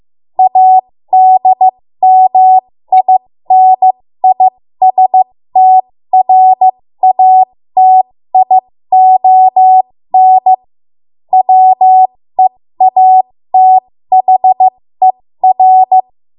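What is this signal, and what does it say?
Morse code, heard as a keyed tone, about 15 wpm, spaced as for 13 wpm, 750 Hz. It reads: ADMINISTRATION WEATHER
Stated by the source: 13 wpm Morse code practice archive transmission.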